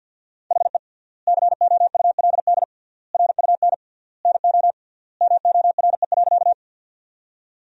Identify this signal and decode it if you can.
Morse code, heard as a keyed tone, about 50 wpm, 700 Hz. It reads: HE COULD RUN NO MORE.
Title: 3x Word-Spacing - Sentences from Top 300 Words - 50wpm